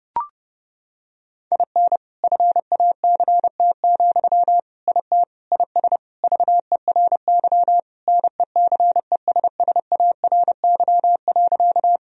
Morse code, heard as a keyed tone, about 32 wpm, 700 Hz. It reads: IN FACT, IT IS VERY NECESSARY.
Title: Sentences from Top 1000 Words - Review Part 3 - 30wpm